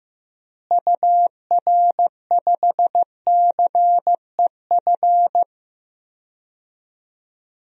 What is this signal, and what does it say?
Morse code, heard as a keyed tone, about 15 wpm, 700 Hz. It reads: UR5CEF
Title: DXCC-100 Call Signs - 6 Characters - Repeated 3x - 15wpm